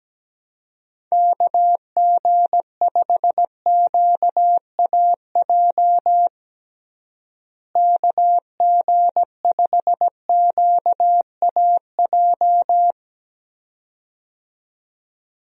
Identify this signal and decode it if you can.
Morse code, heard as a keyed tone, about 17 wpm, 700 Hz. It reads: KG5QAJ KG5QAJ
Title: USA Call Signs - 6 Characters - Repeated 2x - 17wpm